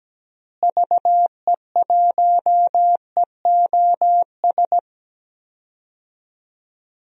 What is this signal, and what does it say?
Morse code, heard as a keyed tone, about 17 wpm, 700 Hz. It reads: VE1EOS
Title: DXCC-100 Call Signs - 6 Characters - Repeated 3x - 17wpm